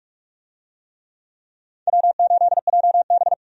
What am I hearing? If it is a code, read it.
Morse code, heard as a keyed tone, about 44 wpm, 700 Hz. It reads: W8JB
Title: USA Call Signs - 4 Characters - 45wpm